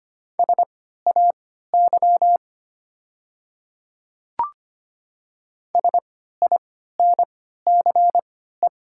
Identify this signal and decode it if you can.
Morse code, heard as a keyed tone, about 25 wpm, spaced as for 14 wpm, 700 Hz. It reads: SAY SINCE
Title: 3x Character Spacing - Top 300 Words - 25wpm